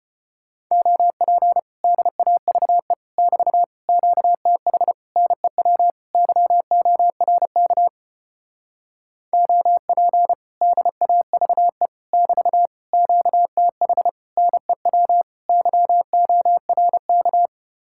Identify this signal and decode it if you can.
Morse code, heard as a keyed tone, about 33 wpm, 700 Hz. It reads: OP DAVE <BT> QTH NEW YORK OP DAVE <BT> QTH NEW YORK